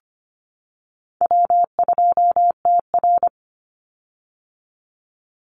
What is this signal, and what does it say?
Morse code, heard as a keyed tone, about 25 wpm, 700 Hz. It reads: W2TR
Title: Call Signs - 25wpm